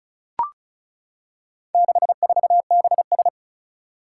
Morse code, heard as a keyed tone, about 36 wpm, 700 Hz. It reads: 64BS
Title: Four Challenging Letter-Number Combinations 35wpm